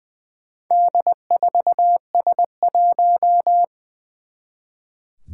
Morse code, heard as a keyed tone, about 20 wpm, 700 Hz. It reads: D4S1
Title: Four Challenging Letter-Number Combinations 20wpm